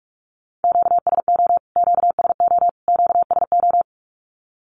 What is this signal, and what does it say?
Morse code, heard as a keyed tone, about 44 wpm, 700 Hz. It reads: QSO QSO QSO
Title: Q-Codes Repeated 3x 45wpm